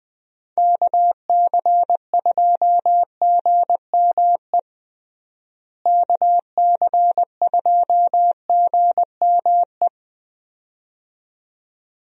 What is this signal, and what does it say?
Morse code, heard as a keyed tone, about 21 wpm, 700 Hz. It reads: KC2GME KC2GME
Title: USA Call Signs - 6 Characters - Repeated 2x - 20wpm